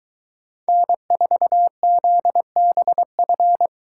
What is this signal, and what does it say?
Morse code, heard as a keyed tone, about 24 wpm, 700 Hz. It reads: N4ZBF